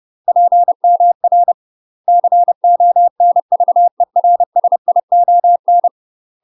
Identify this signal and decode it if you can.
Morse code, heard as a keyed tone, about 30 wpm, 700 Hz. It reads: PMR CONVERSION
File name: UK MASTER QSOs TRAINING / 30WPM / UK MASTER QSOs 0007